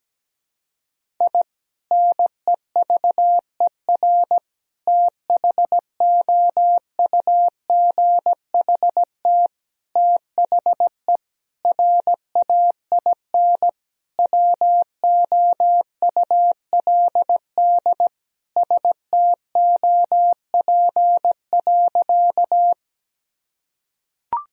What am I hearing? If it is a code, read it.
Morse code, heard as a keyed tone, about 17 wpm, 700 Hz. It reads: I NEVER THOUGHT THE RAIN WOULD STOP.
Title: Sentences from Top 1000 Words - Review Part 1 - 17wpm